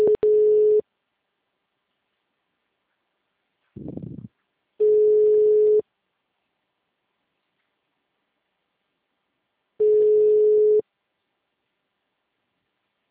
Ringback tone (Europe, 425 Hz).
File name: out- -2013- - - .19567